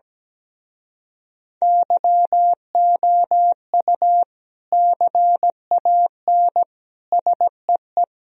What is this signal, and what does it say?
Morse code, heard as a keyed tone, about 17 wpm, 700 Hz. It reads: YOU CAN SEE
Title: Speed-Racing - Sets of 3 Words 17wpm